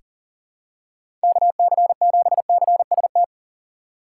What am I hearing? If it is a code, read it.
Morse code, heard as a keyed tone, about 37 wpm, 700 Hz. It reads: KC7CST
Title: USA Call Signs - 6 Characters - 8x Character Spacing - 40wpm